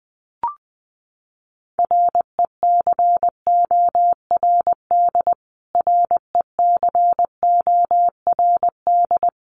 Morse code, heard as a keyed tone, about 20 wpm, 700 Hz. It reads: RECORD RECORD
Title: Top 500 Words Repeated 20wpm